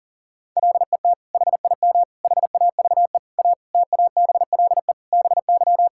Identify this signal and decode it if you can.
Morse code, heard as a keyed tone, about 40 wpm, 700 Hz. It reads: LET HIM HAVE A TABLE BY
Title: Sentences from Top 1600 Words 40wpm